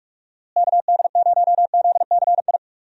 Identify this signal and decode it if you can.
Morse code, heard as a keyed tone, about 41 wpm, 700 Hz. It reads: KD0ZCI
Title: USA Call Signs - 6 Characters - Repeated 3x - 45wpm